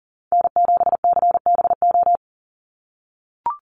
Morse code, heard as a keyed tone, about 40 wpm, 700 Hz. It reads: N7CBO
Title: USA Call Signs 40wpm